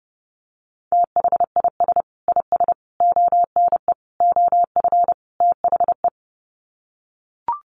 Morse code, heard as a keyed tone, about 30 wpm, 700 Hz. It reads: THIS IS ONE OF THE
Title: Sets of 5 Words 30wpm